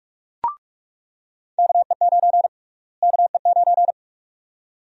Morse code, heard as a keyed tone, about 41 wpm, 700 Hz. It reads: KE9 KE9
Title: USA Call Sign Prefixes - Repeated 2x - 45wpm